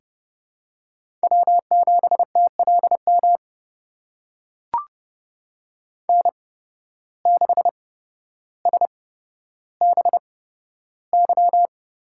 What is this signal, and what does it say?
Morse code, heard as a keyed tone, about 30 wpm, spaced as for 13 wpm, 700 Hz. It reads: W7TLM N6SBY
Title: USA Call Signs - 8x Character Spacing - 30wpm